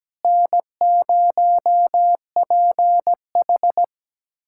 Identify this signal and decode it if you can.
Morse code, heard as a keyed tone, about 17 wpm, 700 Hz. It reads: N0PH
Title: USA Callsign DE Callsign - 4 Characters - 17wpm